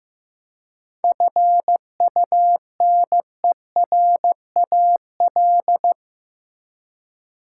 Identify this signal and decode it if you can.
Morse code, heard as a keyed tone, about 15 wpm, 700 Hz. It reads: FUNERAL
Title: Seven Letter Words - 15wpm